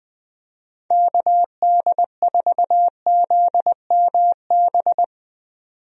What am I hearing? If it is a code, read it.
Morse code, heard as a keyed tone, about 20 wpm, 700 Hz. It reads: KD4ZMB
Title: USA Call Signs - 6 Characters - 20wpm